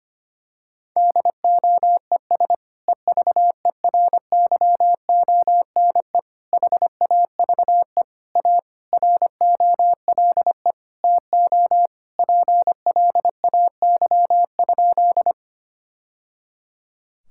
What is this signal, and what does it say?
Morse code, heard as a keyed tone, about 25 wpm, 700 Hz. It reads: DOES EVERYONE HAVE A ROLE TO PLAY?
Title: Sentences from Top 1000 Words - Review Part 6 - 25wpm